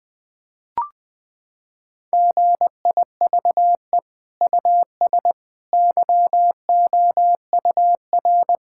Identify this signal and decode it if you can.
Morse code, heard as a keyed tone, about 19 wpm, 700 Hz. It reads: GIVE US YOUR
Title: Sentences from Top 1000 Words - Review Part 5 - 20wpm